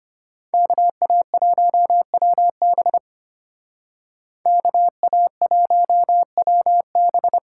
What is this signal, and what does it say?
Morse code, heard as a keyed tone, about 30 wpm, 700 Hz. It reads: KA1WB KA1WB
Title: Speed Racing - USA 2x2 Call Signs 25wpm